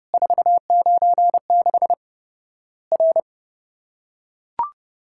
Morse code, heard as a keyed tone, about 30 wpm, 700 Hz. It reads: 496 R
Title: Worked All Europe DX Contest - QTC Exchange - 30wpm